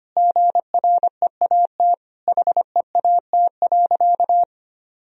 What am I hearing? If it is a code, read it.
Morse code, heard as a keyed tone, about 25 wpm, 700 Hz. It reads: GREAT HEAT.
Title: Sentences from Top 1000 Words - Review Part 3 - 25wpm